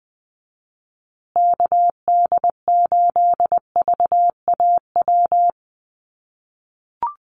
Morse code, heard as a keyed tone, about 20 wpm, 700 Hz. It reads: KD8VAW
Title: USA Call Signs Repeated 3x 20wpm